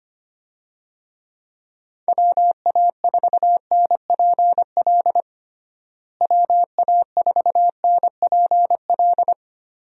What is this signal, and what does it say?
Morse code, heard as a keyed tone, about 25 wpm, 700 Hz. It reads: WA4NPL WA4NPL
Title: USA Call Signs - 6 Characters - Repeated 3x - 25wpm